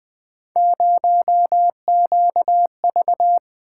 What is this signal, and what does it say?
Morse code, heard as a keyed tone, about 20 wpm, 700 Hz. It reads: 0QV
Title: Three Letter-Number Combinations - 20wpm